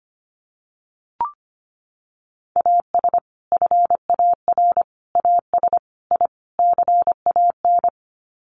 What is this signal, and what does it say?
Morse code, heard as a keyed tone, about 25 wpm, 700 Hz. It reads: AS FAR AS I CAN
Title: Sets of 5 Words Encore 25wpm